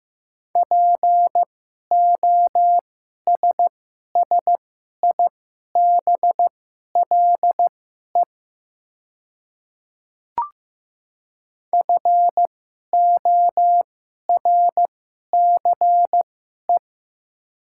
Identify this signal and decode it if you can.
Morse code, heard as a keyed tone, about 15 wpm, spaced as for 11 wpm, 700 Hz. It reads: POSSIBLE FORCE
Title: 2x Character Spacing - Top 500 Words - 15wpm